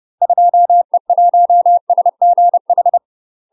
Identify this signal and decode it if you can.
Morse code, heard as a keyed tone, about 30 wpm, 700 Hz. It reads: 2E1SGH
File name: UK MASTER QSOs TRAINING / 30WPM / UK MASTER QSOs 0009